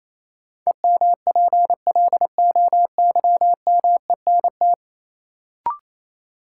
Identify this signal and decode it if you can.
Morse code, heard as a keyed tone, about 28 wpm, 700 Hz. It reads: EMPLOYMENT
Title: Sentences from Top 1500 Words - New Word Drill - 28wpm